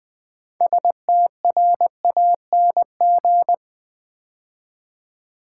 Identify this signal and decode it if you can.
Morse code, heard as a keyed tone, about 20 wpm, 700 Hz. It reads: STRANG
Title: Build A Word - 2 Letters At A Time - 8-Letter Words - 20wpm